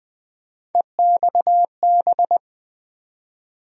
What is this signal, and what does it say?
Morse code, heard as a keyed tone, about 20 wpm, 700 Hz. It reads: EXB
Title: USA Call Sign Postfixes - 3 Letters 20wpm